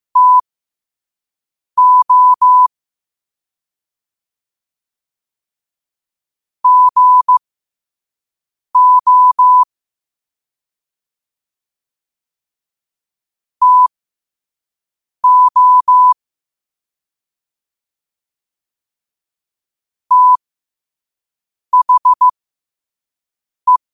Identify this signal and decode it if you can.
Morse code, heard as a keyed tone, about 15 wpm, spaced as for 5 wpm, 1000 Hz. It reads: TO GO TO THE
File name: qotdfiles / 5wpm